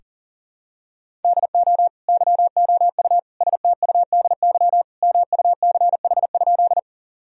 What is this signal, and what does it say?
Morse code, heard as a keyed tone, about 37 wpm, 700 Hz. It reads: DO YOU STUDY MUCH?
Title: Sentences from Top 1000 Words - Review Part 1 - 40wpm